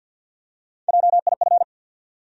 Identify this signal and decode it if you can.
Morse code, heard as a keyed tone, about 53 wpm, 700 Hz. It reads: JIF
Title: USA Call Sign Postfixes - 3 Letters 50wpm